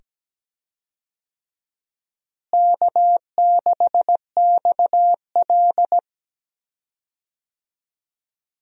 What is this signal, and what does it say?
Morse code, heard as a keyed tone, about 17 wpm, 700 Hz. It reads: K6XL
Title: USA Call Signs - 4 Characters - Repeated 2x - 17wpm